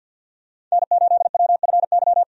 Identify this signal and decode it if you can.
Morse code, heard as a keyed tone, about 52 wpm, 700 Hz. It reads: N8WFY
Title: USA Call Signs - 5 Characters - 50wpm